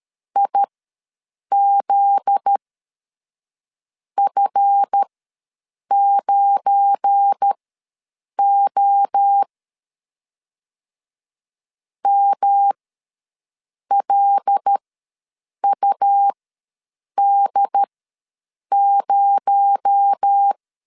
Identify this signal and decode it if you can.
Morse code, heard as a keyed tone, about 13 wpm, spaced as for 6 wpm, 800 Hz. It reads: IZF9O MLUD0
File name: LEZIONI / 7 GF9Z / 7 GF9Z ws 12wpm 3321